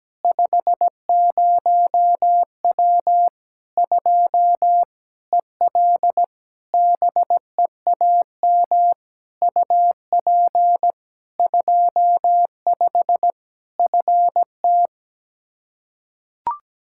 Morse code, heard as a keyed tone, about 17 wpm, 700 Hz. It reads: 50W 2 EL BEAM UP 25 FT
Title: K1USN SST QSO Practice 17wpm